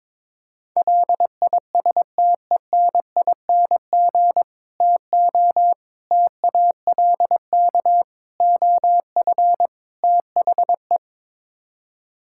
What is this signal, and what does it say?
Morse code, heard as a keyed tone, about 22 wpm, 700 Hz. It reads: LISTENING TO TALK OF THE
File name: practice / 22wpm / Build A Phrase - 5-Word Phrases - 22wpm